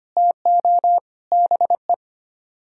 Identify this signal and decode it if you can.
Morse code, heard as a keyed tone, about 33 wpm, 700 Hz. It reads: TO BE